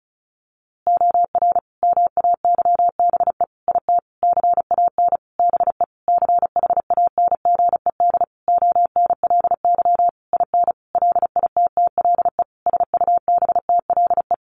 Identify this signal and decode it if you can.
Morse code, heard as a keyed tone, about 34 wpm, 700 Hz. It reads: OR MAYBE IT CAN BE CHANGED ONLY IN LITTLE SUBTLE
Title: A Sound Of Thunder 35wpm